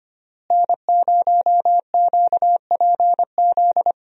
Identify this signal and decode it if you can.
Morse code, heard as a keyed tone, about 25 wpm, 700 Hz. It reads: N0QPZ
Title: USA Call Signs - 5 Characters - Repeated 3x - 25wpm